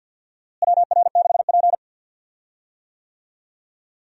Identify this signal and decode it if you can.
Morse code, heard as a keyed tone, about 53 wpm, 700 Hz. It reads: WR6P
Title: USA Call Signs - 4 Characters - Repeated 2x - 50wpm